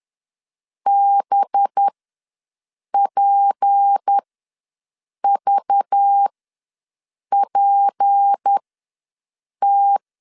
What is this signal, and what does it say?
Morse code, heard as a keyed tone, about 10 wpm, spaced as for 6 wpm, 800 Hz. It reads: BPVPT
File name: LEZIONI / 6 PKX0 / 6 P ws 10wpm 3751